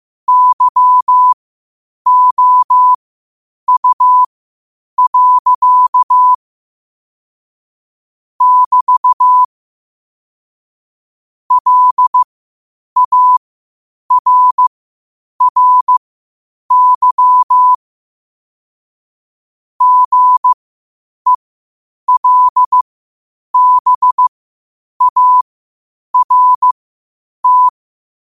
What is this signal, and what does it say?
Morse code, heard as a keyed tone, about 15 wpm, spaced as for 8 wpm, 1000 Hz. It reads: YOU. <BT> LARRY GELBART
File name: qotdfiles / 8wpm